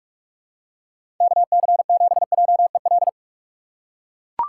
Morse code, heard as a keyed tone, about 42 wpm, 700 Hz. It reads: KC7JEL
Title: USA Call Signs - 4x Character Spacing - 45wpm